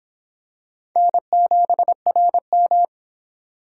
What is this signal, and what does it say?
Morse code, heard as a keyed tone, about 24 wpm, 700 Hz. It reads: N7RM